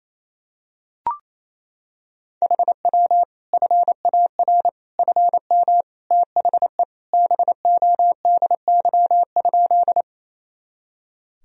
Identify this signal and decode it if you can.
Morse code, heard as a keyed tone, about 28 wpm, 700 Hz. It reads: HW FAR FM THE BODY?
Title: Sentences from Top 400 Words and Abbreviations 28wpm